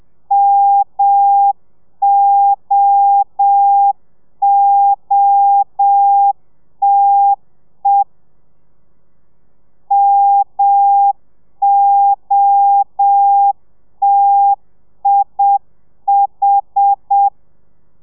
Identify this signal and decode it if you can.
Morse code, heard as a keyed tone, about 7 wpm, 800 Hz. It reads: MOOTE MOTIH
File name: telegrafia / i2tfj / Lez1 30